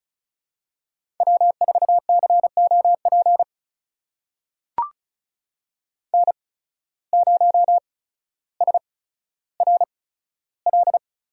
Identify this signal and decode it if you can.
Morse code, heard as a keyed tone, about 36 wpm, spaced as for 15 wpm, 700 Hz. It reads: W4COP N0SRL
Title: USA Call Signs - 5 Characters - 8x Character Spacing - 35wpm